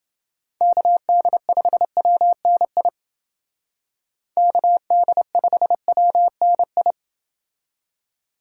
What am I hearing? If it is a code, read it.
Morse code, heard as a keyed tone, about 29 wpm, 700 Hz. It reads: KD5WNI KD5WNI